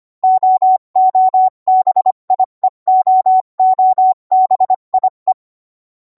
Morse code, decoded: OOBIE OOBIE